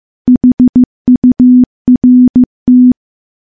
Morse code, heard as a keyed tone, about 15 wpm, 250 Hz. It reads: HURT
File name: morse / morse-training-15WPM